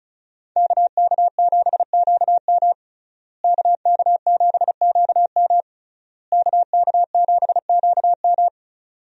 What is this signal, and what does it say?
Morse code, heard as a keyed tone, about 36 wpm, 700 Hz. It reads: KK7QM KK7QM KK7QM
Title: USA Call Signs - 5 Characters - Repeated 3x - 35wpm